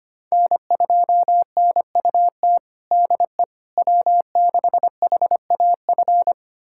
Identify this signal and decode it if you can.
Morse code, heard as a keyed tone, about 25 wpm, 700 Hz. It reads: N2NUT DE W6HAF
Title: USA Callsign DE Callsign - 5 Characters - 25wpm